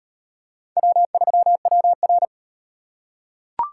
Morse code, heard as a keyed tone, about 38 wpm, 700 Hz. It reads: W3WR